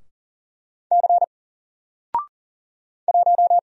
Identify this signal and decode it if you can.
Morse code, heard as a keyed tone, about 40 wpm, 700 Hz. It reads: C 1
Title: Single Character (Letter:Number:Punctuation:Prosign) 40wpm